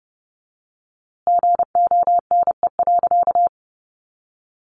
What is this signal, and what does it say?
Morse code, heard as a keyed tone, about 30 wpm, 700 Hz. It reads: GONE.
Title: A Sound Of Thunder 30wpm